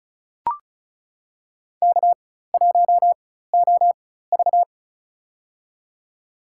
Morse code, heard as a keyed tone, about 36 wpm, spaced as for 16 wpm, 700 Hz. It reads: K1OV
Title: USA Call Signs - 4 Characters - 4x Character Spacing - 35wpm